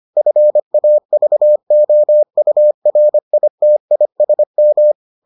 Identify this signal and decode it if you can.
Morse code, heard as a keyed tone, about 25 wpm, 600 Hz. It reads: FAVOURITISM